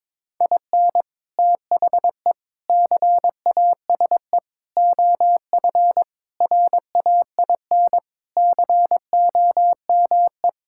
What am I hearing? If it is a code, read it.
Morse code, heard as a keyed tone, about 22 wpm, 700 Hz. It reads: IN THE CASE OF RAIN COME
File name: practice / 22wpm / Sentences with 4-Word Phrases - 22wpm